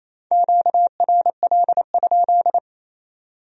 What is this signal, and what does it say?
Morse code, heard as a keyed tone, about 36 wpm, 700 Hz. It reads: QRL?